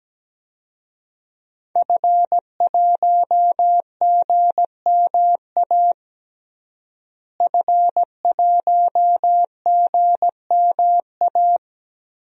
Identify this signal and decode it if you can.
Morse code, heard as a keyed tone, about 17 wpm, 700 Hz. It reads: F1GMA F1GMA
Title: DXCC-100 Call Signs - 5 Characters - Repeated 3x - 17wpm